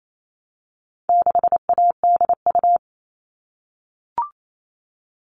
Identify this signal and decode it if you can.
Morse code, heard as a keyed tone, about 28 wpm, 700 Hz. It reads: 6ADU
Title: All Cut Numbers - 4-Digit 28wpm